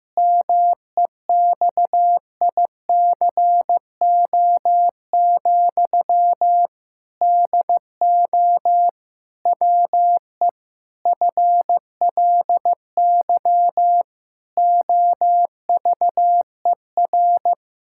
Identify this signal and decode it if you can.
Morse code, heard as a keyed tone, about 14 wpm, 700 Hz. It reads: MEXICO, DO WE FLY OVER
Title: Pangram Challenge v2 - 15wpm